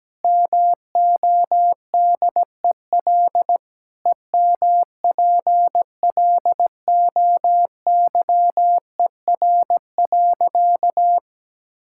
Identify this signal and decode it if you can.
Morse code, heard as a keyed tone, about 17 wpm, 700 Hz. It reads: MODEL EMPLOYER.
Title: Sentences from Top 1600 Words 17wpm